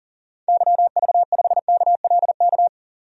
Morse code, heard as a keyed tone, about 38 wpm, 700 Hz. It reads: YV5KLK
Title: DXCC-100 Call Signs - 6 Characters - 40wpm